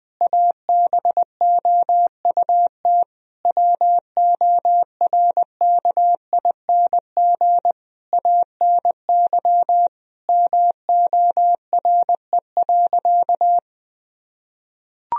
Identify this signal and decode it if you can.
Morse code, heard as a keyed tone, about 21 wpm, 700 Hz. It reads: ABOUT WORKING ANY MORE.
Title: Sentences from Top 1000 Words - Review Part 4 - 20wpm